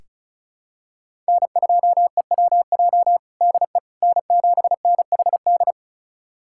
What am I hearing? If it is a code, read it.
Morse code, heard as a keyed tone, about 36 wpm, 700 Hz. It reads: N2EWJ DE N7NHD
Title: USA Callsign DE Callsign - 5 Characters - 35wpm